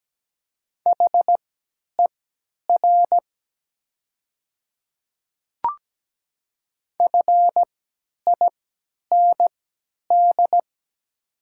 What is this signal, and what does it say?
Morse code, heard as a keyed tone, about 17 wpm, spaced as for 10 wpm, 700 Hz. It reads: HER FIND
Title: 3x Character Spacing - Top 100 Words - 17wpm